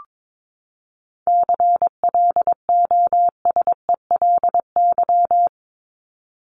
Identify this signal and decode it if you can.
Morse code, heard as a keyed tone, about 22 wpm, 700 Hz. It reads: CLOSELY
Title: Top 1251-1500 Words 22wpm